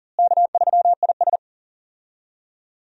Morse code, heard as a keyed tone, about 37 wpm, 700 Hz. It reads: K3IS
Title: USA Call Signs - 4 Characters - Repeated 2x - 40wpm